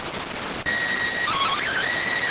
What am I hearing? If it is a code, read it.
SSTV (slow-scan television), PD 290.